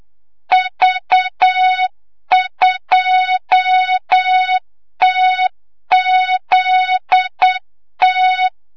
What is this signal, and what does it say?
Morse code, heard as a keyed tone, about 8 wpm, 3700 Hz, with noise. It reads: V2TZT
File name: Morze-HA2MN / hang / 25 GYAKORLAT